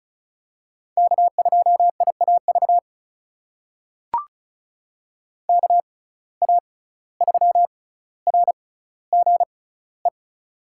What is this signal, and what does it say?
Morse code, heard as a keyed tone, about 35 wpm, spaced as for 17 wpm, 700 Hz. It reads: K2IAV KA3RGE
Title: USA Call Signs - 6x Character Spacing - 35wpm